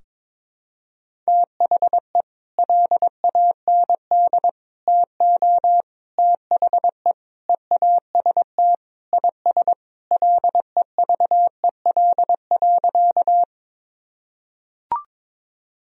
Morse code, heard as a keyed tone, about 22 wpm, 700 Hz. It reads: THE LAND TO THE EAST IS LEVEL.
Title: Sentences from Top 1000 Words - Review Part 4 - 22wpm